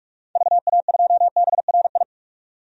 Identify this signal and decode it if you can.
Morse code, heard as a keyed tone, about 42 wpm, 700 Hz. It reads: VA2BRI
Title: DXCC-100 Call Signs - 6 Characters - 45wpm